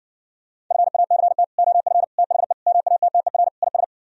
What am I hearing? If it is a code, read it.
Morse code, heard as a keyed tone, about 60 wpm, 700 Hz. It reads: FACT OF THE MATTER IS